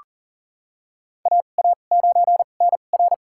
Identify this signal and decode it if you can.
Morse code, heard as a keyed tone, about 40 wpm, spaced as for 29 wpm, 700 Hz. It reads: AA9NR
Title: USA Call Signs - 5 Characters - 2x Character Spacing - 40wpm